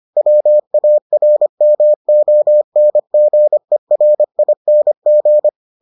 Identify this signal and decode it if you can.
Morse code, heard as a keyed tone, about 25 wpm, 600 Hz. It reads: WARMONGERING